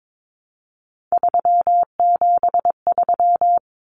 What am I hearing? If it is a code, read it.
Morse code, heard as a keyed tone, about 22 wpm, 700 Hz. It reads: 373
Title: Numbers - 3-Digit 22wpm